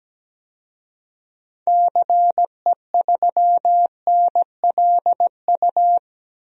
Morse code, heard as a keyed tone, about 17 wpm, 700 Hz. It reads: CE3NLU
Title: DXCC-100 Call Signs - 6 Characters - 17wpm